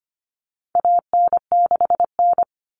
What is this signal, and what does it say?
Morse code, heard as a keyed tone, about 25 wpm, 700 Hz. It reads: AN6N
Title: Common Cut Numbers - 4-Digit 25wpm